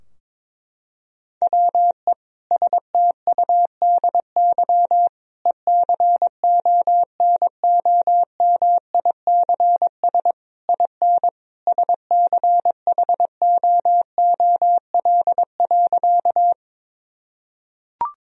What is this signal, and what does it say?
Morse code, heard as a keyed tone, about 22 wpm, 700 Hz. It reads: WE STUDY ECONOMICS IN SCHOOL.
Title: Sentences from Top 1000 Words - Review Part 3 - 22wpm